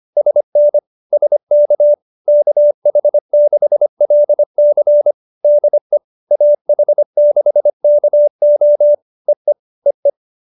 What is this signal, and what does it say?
Morse code, heard as a keyed tone, about 25 wpm, 590 Hz. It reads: SN SK KH6LC DE AH6KO EE EE